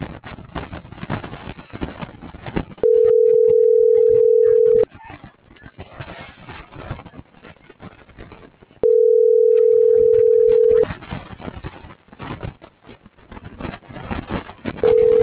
Ringback tone (North America), 440 + 480 Hz.